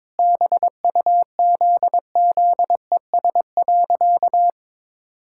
Morse code, heard as a keyed tone, about 22 wpm, 700 Hz. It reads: BUZZES.
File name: practice / 22wpm / Sentences with 4-Word Phrases - 22wpm